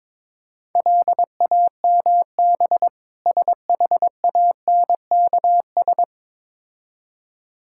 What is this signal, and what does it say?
Morse code, heard as a keyed tone, about 22 wpm, 700 Hz. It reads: LAMB SHANKS
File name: Difficult Words with Hints - 22wpm